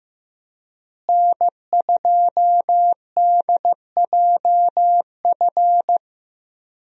Morse code, heard as a keyed tone, about 15 wpm, 700 Hz. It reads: N2DJF
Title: USA Call Signs - 5 Characters - Repeated 3x - 15wpm